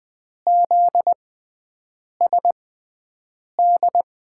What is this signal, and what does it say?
Morse code, heard as a keyed tone, about 20 wpm, spaced as for 7 wpm, 700 Hz. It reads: ZSD
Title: USA Call Signs - 5 Characters - 6x Character Spacing - 20wpm